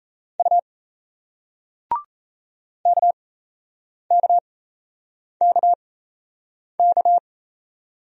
Morse code, heard as a keyed tone, about 38 wpm, spaced as for 8 wpm, 700 Hz. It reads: U KKKK